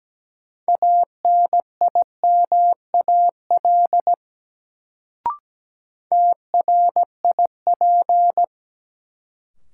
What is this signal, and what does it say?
Morse code, heard as a keyed tone, about 17 wpm, 700 Hz. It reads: ANIMAL TRIP
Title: Sentences from Top 1300 Words - New Word Drill Alt - 17wpm